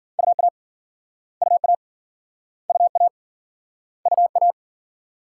Morse code, heard as a keyed tone, about 53 wpm, 700 Hz. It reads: UA UA UA UA